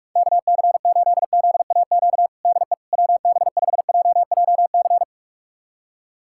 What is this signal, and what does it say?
Morse code, heard as a keyed tone, about 41 wpm, 700 Hz. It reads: KC8ZAQ DE WB5JJC
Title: USA Callsign DE Callsign 45wpm